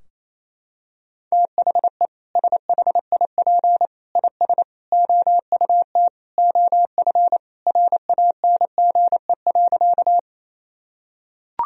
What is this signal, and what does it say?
Morse code, heard as a keyed tone, about 28 wpm, 700 Hz. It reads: THE SHIP IS OUT OF RANGE.
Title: Sentences from Top 1000 Words - Review Part 4 - 28wpm